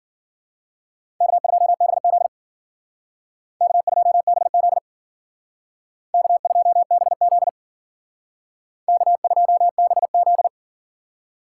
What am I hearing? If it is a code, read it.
Morse code, heard as a keyed tone, about 53 wpm, 700 Hz. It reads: K2BZ K2BZ K2BZ K2BZ